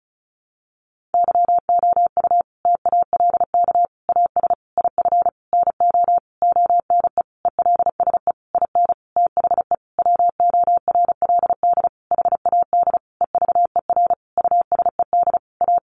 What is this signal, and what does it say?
Morse code, heard as a keyed tone, about 34 wpm, 700 Hz. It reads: YOU TALK AS IF NO ONE ELSE IN THE WORLD HAD EVER USED A
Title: Sentences from Top 1100 Words 35wpm